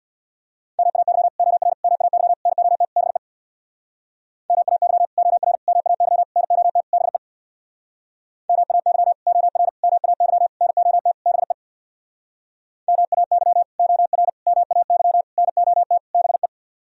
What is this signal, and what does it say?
Morse code, heard as a keyed tone, about 62 wpm, 700 Hz. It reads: MAY OR MAY NOT BE MAY OR MAY NOT BE MAY OR MAY NOT BE MAY OR MAY NOT BE
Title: Speed-Racing - Sets of 5 Words 50wpm